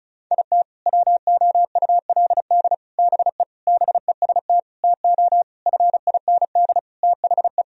Morse code, heard as a keyed tone, about 36 wpm, 700 Hz. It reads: IT WOULD BE BEST TO FIND THE
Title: Sentences from Top 1000 Words - Review Part 2 - 35wpm